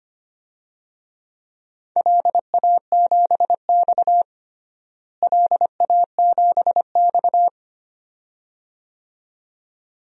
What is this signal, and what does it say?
Morse code, heard as a keyed tone, about 25 wpm, 700 Hz. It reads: LA7X LA7X
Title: DXCC-100 Call Signs - 4 Characters - Repeated 2x - 25wpm